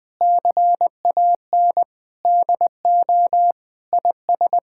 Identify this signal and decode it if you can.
Morse code, heard as a keyed tone, about 20 wpm, 700 Hz. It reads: CAN DO IS
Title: Sentences with 7-Word Phrases - Spoken Beginning - 20wpm